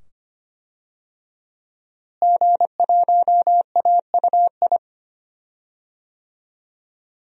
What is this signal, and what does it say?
Morse code, heard as a keyed tone, about 25 wpm, 700 Hz. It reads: G1AUI
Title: DXCC-100 Call Signs - 5 Characters - Repeated 2x - 25wpm